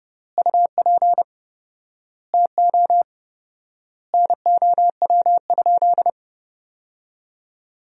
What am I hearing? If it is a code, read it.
Morse code, heard as a keyed tone, about 30 wpm, 700 Hz. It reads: UP TO NOW?
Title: 4x Word-Spacing - Sentences from Top 100 Words - 30wpm